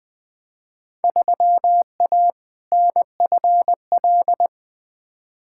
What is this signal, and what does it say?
Morse code, heard as a keyed tone, about 21 wpm, 700 Hz. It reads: 3A NFL
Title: Field Day Exchanges - 20wpm